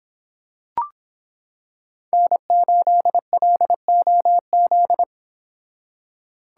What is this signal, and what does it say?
Morse code, heard as a keyed tone, about 25 wpm, 700 Hz. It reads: N8LOZ